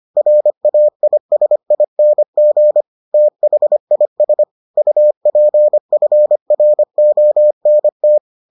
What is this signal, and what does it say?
Morse code, heard as a keyed tone, about 25 wpm, 600 Hz. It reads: RAISING THIS UPFRONT